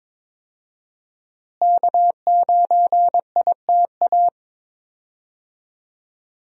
Morse code, heard as a keyed tone, about 22 wpm, 700 Hz. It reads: K9ITA
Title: USA Call Signs - 5 Characters - Repeated 2x - 22wpm